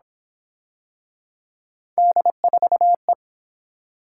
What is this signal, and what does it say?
Morse code, heard as a keyed tone, about 29 wpm, 700 Hz. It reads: D4E